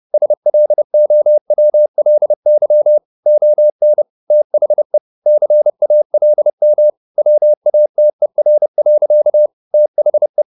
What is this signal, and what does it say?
Morse code, heard as a keyed tone, about 30 wpm, 600 Hz. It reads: SLOWLY ON THE CALM WATER. THE
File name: the old man and the sea / the old man and the sea 1 30wpm